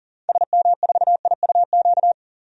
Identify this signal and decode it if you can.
Morse code, heard as a keyed tone, about 40 wpm, 700 Hz. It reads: SM4IUQ
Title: DXCC-100 Call Signs - 6 Characters - 40wpm